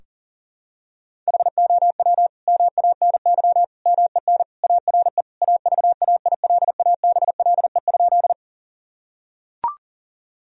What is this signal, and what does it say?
Morse code, heard as a keyed tone, about 40 wpm, 700 Hz. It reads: HOW MANY MEN ARE AVAILABLE?
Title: Sentences from Top 1000 Words - Review Part 3 - 40wpm